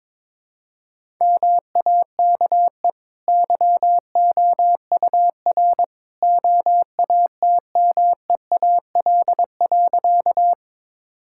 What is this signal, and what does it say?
Morse code, heard as a keyed tone, about 22 wpm, 690 Hz. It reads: MAKE YOUR OATMEAL.
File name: practice / 22wpm / Easy Sentences with Hints - 22wpm